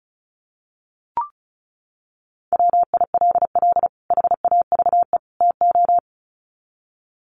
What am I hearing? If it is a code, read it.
Morse code, heard as a keyed tone, about 34 wpm, 700 Hz. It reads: WILL HAVE TO
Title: Sets of 3 Words 35wpm